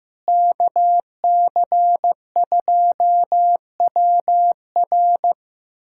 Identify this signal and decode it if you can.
Morse code, heard as a keyed tone, about 16 wpm, 700 Hz. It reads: KC2WR
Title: USA Callsign DE Callsign - 5 Characters - 15wpm